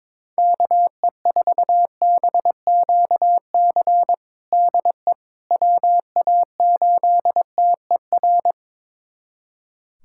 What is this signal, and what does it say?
Morse code, heard as a keyed tone, about 22 wpm, 700 Hz. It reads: KE4BQC DE WA8TER
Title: USA Callsign DE Callsign 22wpm